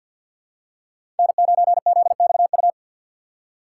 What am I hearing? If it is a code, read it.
Morse code, heard as a keyed tone, about 48 wpm, 700 Hz. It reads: N9ZXU
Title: USA Call Signs - 5 Characters - 4x Character Spacing - 50wpm